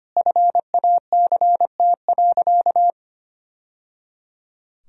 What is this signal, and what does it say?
Morse code, heard as a keyed tone, about 25 wpm, 700 Hz. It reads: FACT.
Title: Sentences from Top 1000 Words - Review Part 3 - 25wpm